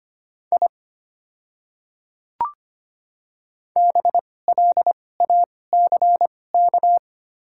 Morse code, heard as a keyed tone, about 25 wpm, spaced as for 18 wpm, 700 Hz. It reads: I BLACK
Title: 2x Character Spacing - Top 500 Words - 25wpm